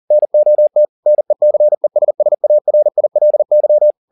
Morse code, heard as a keyed tone, about 40 wpm, 600 Hz. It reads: NOT NECESSARILY